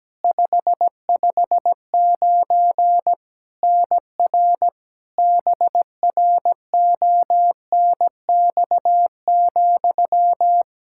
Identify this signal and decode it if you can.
Morse code, heard as a keyed tone, about 17 wpm, 700 Hz. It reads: 559 NR BRONX,
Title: K1USN SST QSO Practice 17wpm